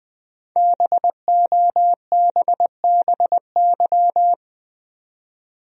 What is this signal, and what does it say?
Morse code, heard as a keyed tone, about 20 wpm, 700 Hz. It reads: BOBBY
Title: USA Common Names - 20wpm